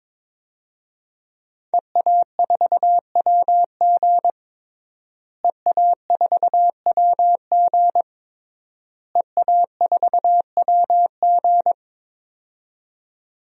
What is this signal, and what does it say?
Morse code, heard as a keyed tone, about 22 wpm, 700 Hz. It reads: EA4WG EA4WG EA4WG
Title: DXCC-100 Call Signs - 5 Characters - Repeated 3x - 22wpm